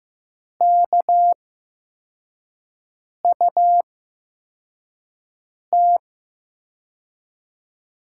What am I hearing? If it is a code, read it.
Morse code, heard as a keyed tone, about 15 wpm, spaced as for 4 wpm, 700 Hz. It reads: KUT
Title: USA Call Signs - 8x Character Spacing - 15wpm